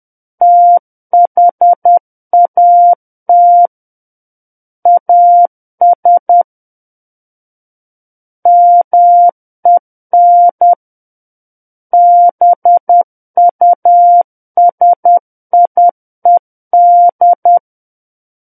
Morse code, heard as a keyed tone, about 10 wpm, 700 Hz. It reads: THAT AS MEN BUSIED
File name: Book One / 01-Chapter 1 - 10 WPM